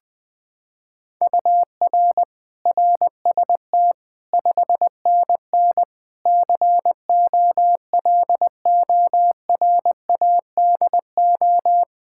Morse code, heard as a keyed tone, about 20 wpm, 700 Hz. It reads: UR RST 5NN COLORADO
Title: Practice QSO - Exchange 1 - Beginning - 20wpm